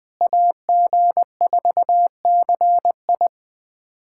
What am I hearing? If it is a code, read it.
Morse code, heard as a keyed tone, about 20 wpm, 700 Hz. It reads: AG4CI
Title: USA Callsign DE Callsign - 5 Characters - 20wpm